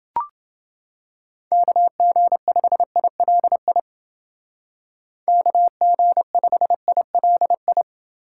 Morse code, heard as a keyed tone, about 31 wpm, 700 Hz. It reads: KG5ILI KG5ILI